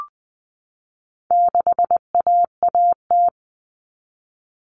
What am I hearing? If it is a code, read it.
Morse code, heard as a keyed tone, about 20 wpm, 700 Hz. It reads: 6AAT
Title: Common Cut Numbers - 4-Digit 20wpm